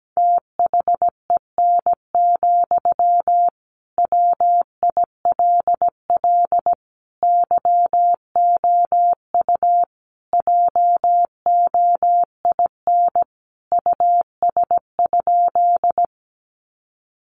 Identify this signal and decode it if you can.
Morse code, heard as a keyed tone, about 17 wpm, 700 Hz. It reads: THEN, WILL YOU JOIN US?
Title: Sentences from Top 1400 Words 17wpm